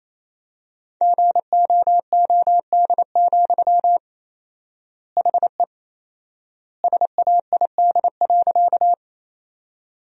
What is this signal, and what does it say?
Morse code, heard as a keyed tone, about 27 wpm, 700 Hz. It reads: GOOD, HE SAID.
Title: 4x Word-Spacing - Sentences from Top 200 Words - 28wpm